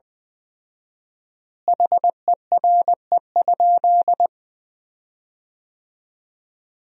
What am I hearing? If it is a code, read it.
Morse code, heard as a keyed tone, about 20 wpm, 700 Hz. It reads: HERE?
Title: 4x Word-Spacing - Sentences from Top 200 Words - 20wpm